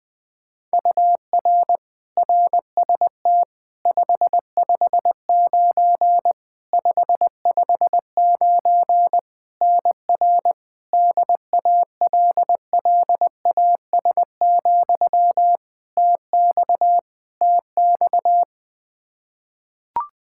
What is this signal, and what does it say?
Morse code, heard as a keyed tone, about 20 wpm, 700 Hz. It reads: UR RST 559 559 NR DALLAS, TX TX
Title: Practice QSO - Exchange 1 - Beginning - 20wpm